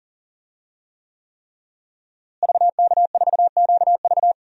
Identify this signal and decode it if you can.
Morse code, heard as a keyed tone, about 40 wpm, 700 Hz. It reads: VK4QV
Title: DXCC-100 Call Signs - 5 Characters - Repeated 3x - 40wpm